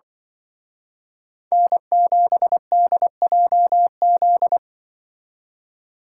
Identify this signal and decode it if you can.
Morse code, heard as a keyed tone, about 24 wpm, 700 Hz. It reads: N7DJZ
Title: Speed Racing - USA 1x3 Call Signs 20wpm